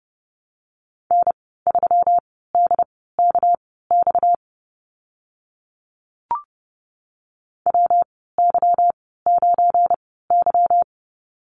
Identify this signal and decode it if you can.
Morse code, heard as a keyed tone, about 30 wpm, spaced as for 17 wpm, 700 Hz. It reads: N3DKX WY9Y
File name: Lesson 37g - Letter x - Call Signs 30wpm